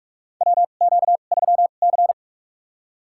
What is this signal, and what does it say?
Morse code, heard as a keyed tone, about 46 wpm, spaced as for 32 wpm, 700 Hz. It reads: WQ3C
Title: USA Call Signs - 2x Character Spacing - 45wpm